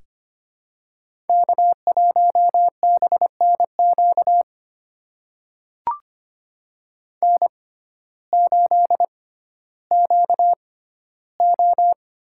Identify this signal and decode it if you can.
Morse code, heard as a keyed tone, about 25 wpm, spaced as for 14 wpm, 700 Hz. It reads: K1BNQ N8QO
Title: USA Call Signs - 6x Character Spacing - 25wpm